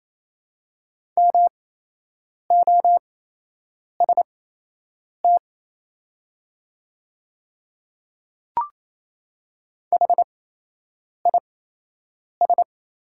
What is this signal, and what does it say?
Morse code, heard as a keyed tone, about 28 wpm, spaced as for 8 wpm, 690 Hz. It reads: MOST HIS
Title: 8x Character Spacing - Top 100 Words - 28wpm